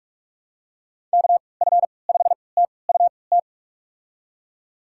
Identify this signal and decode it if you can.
Morse code, heard as a keyed tone, about 46 wpm, spaced as for 26 wpm, 700 Hz. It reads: KF5TUT
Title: USA Call Signs - 6 Characters - 3x Character Spacing - 45wpm